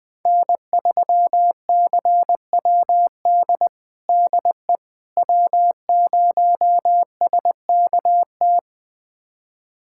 Morse code, heard as a keyed tone, about 21 wpm, 700 Hz. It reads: N3CWD DE W0SKT
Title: USA Callsign DE Callsign - 5 Characters - 20wpm